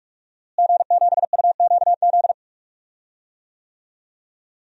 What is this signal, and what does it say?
Morse code, heard as a keyed tone, about 43 wpm, 700 Hz. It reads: G7UQZ
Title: DXCC-100 Call Signs - 5 Characters - Repeated 2x - 45wpm